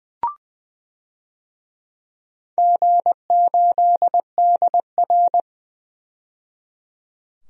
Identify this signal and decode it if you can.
Morse code, heard as a keyed tone, about 21 wpm, 700 Hz. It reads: G8DR
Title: DXCC-100 Call Signs - 4 Characters - 20wpm